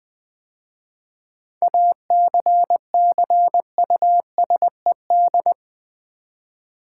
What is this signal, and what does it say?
Morse code, heard as a keyed tone, about 20 wpm, 700 Hz. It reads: ACCUSED
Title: 7 Letter Words - First 3 Letters Given - 20wpm